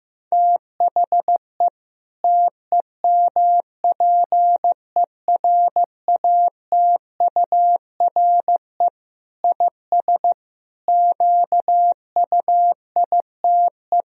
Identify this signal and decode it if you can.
Morse code, heard as a keyed tone, about 14 wpm, 700 Hz. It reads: THE TEMPERATURE IS QUITE
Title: Sentences from Top 1000 Words - Review Part 4 - 15wpm